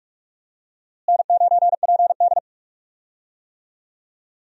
Morse code, heard as a keyed tone, about 46 wpm, 700 Hz. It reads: N9PD
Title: USA Call Signs - 4 Characters - Repeated 3x - 45wpm